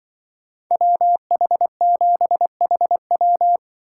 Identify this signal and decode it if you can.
Morse code, heard as a keyed tone, about 24 wpm, 700 Hz. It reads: WH7HW
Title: Speed Racing - USA 2x2 Call Signs 20wpm